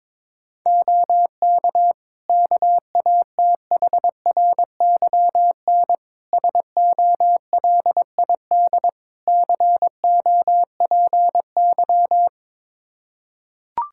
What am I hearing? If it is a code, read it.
Morse code, heard as a keyed tone, about 22 wpm, 700 Hz. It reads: OK KATHRYN SOLID COPY
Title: K1USN SST QSO Practice 22wpm